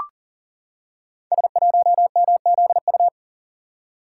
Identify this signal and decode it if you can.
Morse code, heard as a keyed tone, about 39 wpm, 700 Hz. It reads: S1MZU
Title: Code Groups - 5 Letter:Numbers - 40wpm